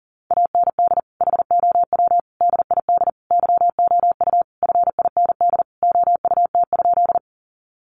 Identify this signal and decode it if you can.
Morse code, heard as a keyed tone, about 40 wpm, 700 Hz. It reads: AND HOW DID YOU FIND OUT?
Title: Sentences from Top 200 Words 40wpm